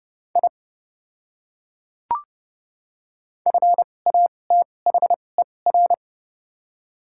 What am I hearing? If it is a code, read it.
Morse code, heard as a keyed tone, about 30 wpm, spaced as for 22 wpm, 700 Hz. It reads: I FATHER
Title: 2x Character Spacing - Top 300 Words - 30wpm